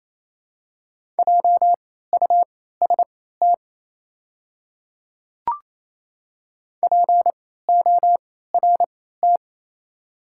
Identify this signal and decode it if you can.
Morse code, heard as a keyed tone, about 28 wpm, spaced as for 16 wpm, 700 Hz. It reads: JUST PORT
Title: 3x Character Spacing - Top 200 Words - 28wpm